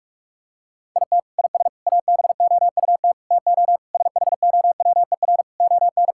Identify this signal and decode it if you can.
Morse code, heard as a keyed tone, about 46 wpm, 700 Hz. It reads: IT IS ABOUT TO SHOWER ON
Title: Sentences from Top 1600 Words 45wpm